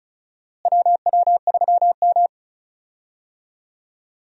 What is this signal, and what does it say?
Morse code, heard as a keyed tone, about 36 wpm, 700 Hz. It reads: WW3M
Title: USA Call Signs - 4 Characters - Repeated 2x - 35wpm